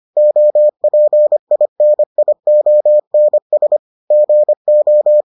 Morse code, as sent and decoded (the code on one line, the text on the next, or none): --- .--. .. -. .. --- -. ... / --. ---
OPINIONS GO